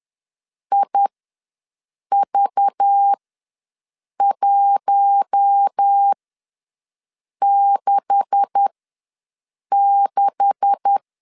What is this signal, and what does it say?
Morse code, heard as a keyed tone, about 10 wpm, spaced as for 6 wpm, 800 Hz. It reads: IV166
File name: LEZIONI / 9 286 ws 10wpm 4032